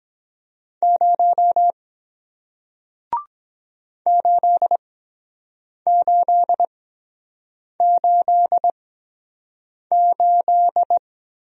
Morse code, heard as a keyed tone, about 23 wpm, spaced as for 7 wpm, 700 Hz. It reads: 0 8888